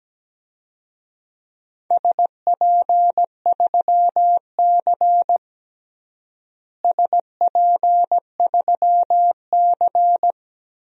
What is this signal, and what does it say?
Morse code, heard as a keyed tone, about 17 wpm, 700 Hz. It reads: SP3C SP3C
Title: DXCC-100 Call Signs - 4 Characters - Repeated 3x - 17wpm